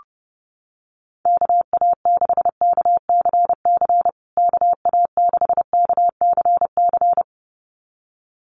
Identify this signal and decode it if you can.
Morse code, heard as a keyed tone, about 30 wpm, 700 Hz. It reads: KA6KCC KA6KCC
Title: USA Call Signs Repeated 2x 30wpm